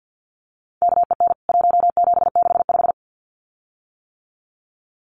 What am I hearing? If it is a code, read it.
Morse code, heard as a keyed tone, about 48 wpm, 700 Hz. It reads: KEN 1765
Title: CWT Exchange 50wpm